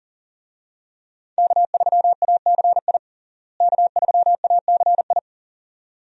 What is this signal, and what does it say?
Morse code, heard as a keyed tone, about 40 wpm, 700 Hz. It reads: K3ACI K3ACI
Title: USA Call Signs - 5 Characters - Repeated 2x - 40wpm